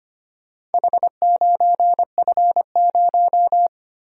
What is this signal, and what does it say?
Morse code, heard as a keyed tone, about 25 wpm, 700 Hz. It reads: H9F0
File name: practice / 25wpm / Four Letter-Number Combinations - 25wpm